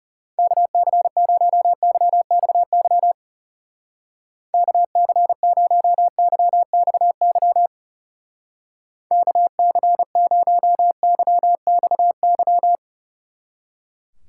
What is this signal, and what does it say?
Morse code, heard as a keyed tone, about 35 wpm, 700 Hz. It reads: KC0YXY KC0YXY KC0YXY
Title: Speed Racing - USA Call Signs 30wpm